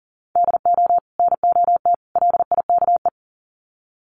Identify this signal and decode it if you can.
Morse code, heard as a keyed tone, about 40 wpm, 700 Hz. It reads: DO NOT LIKE
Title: Sets of 3 Words 40wpm